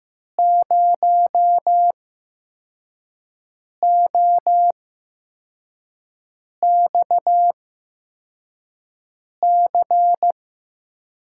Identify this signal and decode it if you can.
Morse code, heard as a keyed tone, about 15 wpm, spaced as for 4 wpm, 700 Hz. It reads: 0OXC
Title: USA Call Signs - 6 Characters - 8x Character Spacing - 15wpm